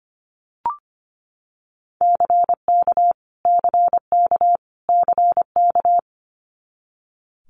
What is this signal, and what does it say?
Morse code, heard as a keyed tone, about 25 wpm, 700 Hz. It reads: CK CK CK
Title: Multi-letter Phonemes Repeated 3x 25wpm